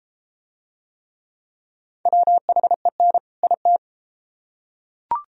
Morse code, heard as a keyed tone, about 28 wpm, 700 Hz. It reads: WHEN IT